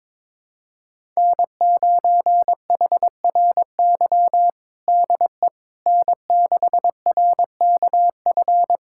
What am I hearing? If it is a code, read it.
Morse code, heard as a keyed tone, about 22 wpm, 700 Hz. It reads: N9HRY DE N6RKF
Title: USA Callsign DE Callsign - 5 Characters - 22wpm